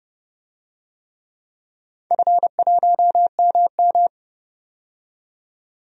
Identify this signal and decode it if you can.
Morse code, heard as a keyed tone, about 30 wpm, 700 Hz. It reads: F1MM
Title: DXCC-100 Call Signs - 4 Characters - 30wpm